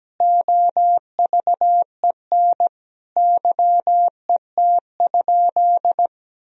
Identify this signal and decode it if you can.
Morse code, heard as a keyed tone, about 17 wpm, 700 Hz. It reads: OVEN YET?
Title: Sentences from Top 1000 Words - Review Part 4 - 17wpm